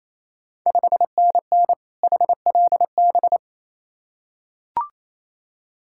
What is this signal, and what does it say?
Morse code, heard as a keyed tone, about 28 wpm, 700 Hz. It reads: 5NN HLB
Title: Ontario QSO Party Exchange 28wpm